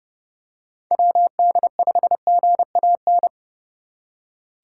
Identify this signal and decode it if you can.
Morse code, heard as a keyed tone, about 29 wpm, 700 Hz. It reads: WD5GAN